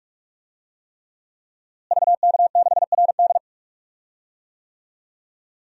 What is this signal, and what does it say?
Morse code, heard as a keyed tone, about 41 wpm, 700 Hz. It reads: VK6RD
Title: DXCC-100 Call Signs - 5 Characters - Repeated 3x - 45wpm